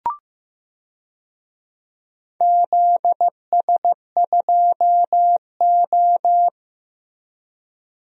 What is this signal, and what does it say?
Morse code, heard as a keyed tone, about 15 wpm, 700 Hz. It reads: ZS2O
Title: DXCC-100 Call Signs - 4 Characters - 15wpm